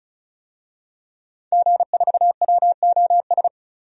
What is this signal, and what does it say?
Morse code, heard as a keyed tone, about 36 wpm, 700 Hz. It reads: G4WOS
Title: DXCC-100 Call Signs - 5 Characters - 35wpm